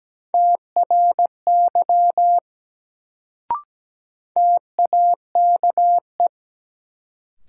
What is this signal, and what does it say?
Morse code, heard as a keyed tone, about 17 wpm, 700 Hz. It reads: TRY TAKE
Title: Sentences from Top 200 Words - New Word Drill - 17wpm